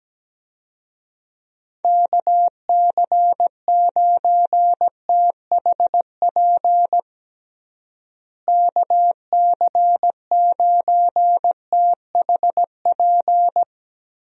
Morse code, heard as a keyed tone, about 17 wpm, 700 Hz. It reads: KC9THP KC9THP
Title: USA Call Signs - 6 Characters - Repeated 2x - 17wpm